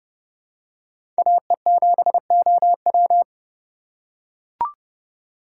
Morse code, heard as a keyed tone, about 31 wpm, 700 Hz. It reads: AE7OW
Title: USA Call Signs - 5 Characters - 2x Character Spacing - 30wpm